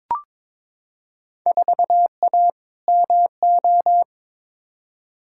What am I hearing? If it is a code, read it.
Morse code, heard as a keyed tone, about 22 wpm, 700 Hz. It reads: 4A MO
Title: Field Day Exchanges - 22wpm